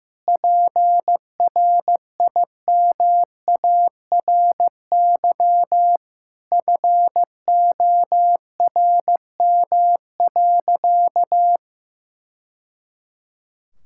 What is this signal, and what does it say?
Morse code, heard as a keyed tone, about 16 wpm, 700 Hz. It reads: PRIMARY FORM.
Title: Sentences from Top 1000 Words - Review Part 5 - 15wpm